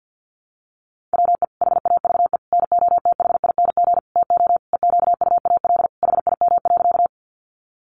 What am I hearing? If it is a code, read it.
Morse code, heard as a keyed tone, about 50 wpm, 700 Hz. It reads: WE HAVE NOTHING TO EQUAL HIM.
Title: Sentences from Top 1400 Words 50wpm